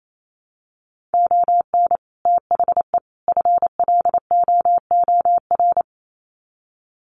Morse code, heard as a keyed tone, about 28 wpm, 700 Hz. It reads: ON THE FLOOR
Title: Sets of 3 Words 28wpm